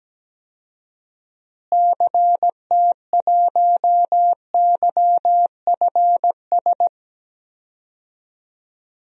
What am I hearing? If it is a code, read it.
Morse code, heard as a keyed tone, about 17 wpm, 700 Hz. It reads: CT1YFS
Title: DXCC-100 Call Signs - 6 Characters - Repeated 3x - 17wpm